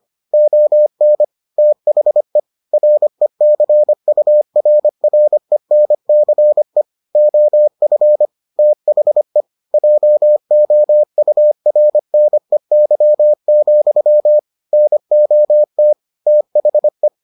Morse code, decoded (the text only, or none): ON THE RECURRENCE OF THE JOURNEY, NOT THE